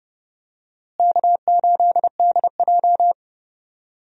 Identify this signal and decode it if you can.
Morse code, heard as a keyed tone, about 32 wpm, 700 Hz. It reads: K8DJ
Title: USA Call Signs - 4 Characters - Repeated 2x - 30wpm